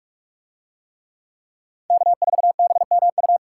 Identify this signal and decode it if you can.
Morse code, heard as a keyed tone, about 44 wpm, 700 Hz. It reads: K4BMU